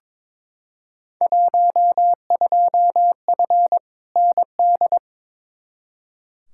Morse code, heard as a keyed tone, about 22 wpm, 700 Hz. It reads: 12F ND
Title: Field Day Exchanges Encore - 22wpm